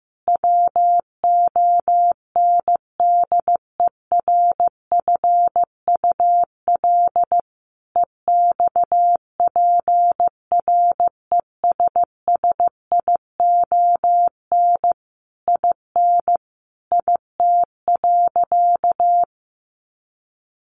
Morse code, heard as a keyed tone, about 15 wpm, 700 Hz. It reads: WONDERFUL EXPRESSION IN IT.
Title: Sentences from Top 1200 Words 15wpm